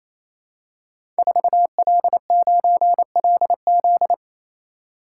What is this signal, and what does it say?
Morse code, heard as a keyed tone, about 29 wpm, 700 Hz. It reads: 4L9LZ
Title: Code Groups - 5 Letter:Numbers - 28wpm